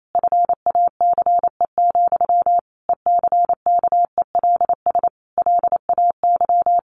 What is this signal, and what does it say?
Morse code, heard as a keyed tone, about 28 wpm, 700 Hz. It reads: FACE, ECKELS LAY
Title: A Sound Of Thunder 28wpm